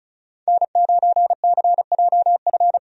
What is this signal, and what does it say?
Morse code, heard as a keyed tone, about 36 wpm, 700 Hz. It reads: N9CJF
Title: USA Call Signs - 5 Characters - 35wpm